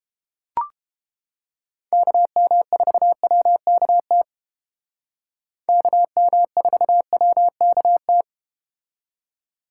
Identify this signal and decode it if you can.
Morse code, heard as a keyed tone, about 32 wpm, 700 Hz. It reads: KM4WKT KM4WKT